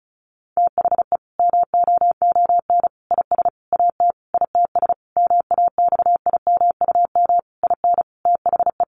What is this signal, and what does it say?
Morse code, heard as a keyed tone, about 34 wpm, 700 Hz. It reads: THE MOON IS AT ITS MAXIMUM IN THE
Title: Sentences from Top 1300 Words 35wpm